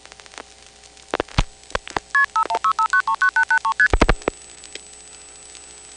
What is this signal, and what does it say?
Multi-frequency trunk signalling, heard as fast pairs of tones, tones of about 90 ms. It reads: KP5165939883ST